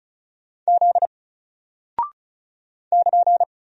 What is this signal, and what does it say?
Morse code, heard as a keyed tone, about 34 wpm, 700 Hz. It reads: Z <KN>
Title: Single Character (Letter:Number:Punctuation:Prosign) 35wpm